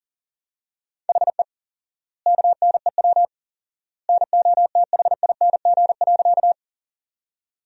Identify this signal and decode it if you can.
Morse code, heard as a keyed tone, about 40 wpm, 700 Hz. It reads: HE KNEW NOTHING.
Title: 4x Word-Spacing - Sentences from Top 500 Words - 40wpm